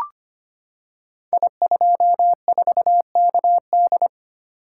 Morse code, heard as a keyed tone, about 25 wpm, 700 Hz. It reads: I24KD
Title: Code Groups - 5 Letter:Numbers - 25wpm